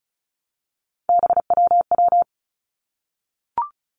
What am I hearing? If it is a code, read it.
Morse code, heard as a keyed tone, about 36 wpm, 700 Hz. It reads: BWW
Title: Three Challenging Letter-Number Combinations 35wpm